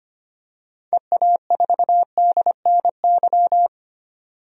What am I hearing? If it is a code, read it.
Morse code, heard as a keyed tone, about 25 wpm, 700 Hz. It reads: EA4DNY
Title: DXCC-100 Call Signs - 6 Characters - 25wpm